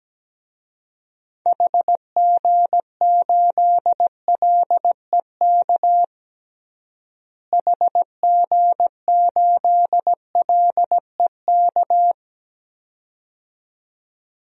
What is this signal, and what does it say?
Morse code, heard as a keyed tone, about 17 wpm, 700 Hz. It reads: HG8LEK HG8LEK
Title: DXCC-100 Call Signs - 6 Characters - Repeated 2x - 17wpm